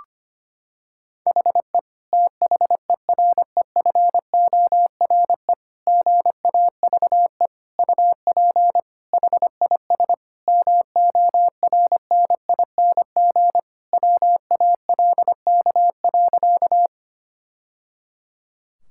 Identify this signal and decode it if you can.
Morse code, heard as a keyed tone, about 25 wpm, 700 Hz. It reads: HE THEREFORE GAVE UP HIS MORNING WALK.
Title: Sentences from Top 1000 Words - Review Part 5 - 25wpm